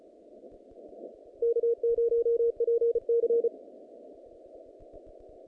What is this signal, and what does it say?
Morse code, heard as a keyed tone, about 34 wpm, 470 Hz. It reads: K0PC
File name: contests / naqp-cw-k5tr-op-k5pi- -2200